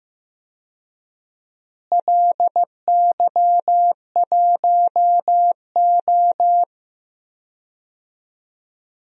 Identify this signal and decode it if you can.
Morse code, heard as a keyed tone, about 15 wpm, 700 Hz. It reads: LY1O
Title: DXCC-100 Call Signs - 4 Characters - Repeated 3x - 15wpm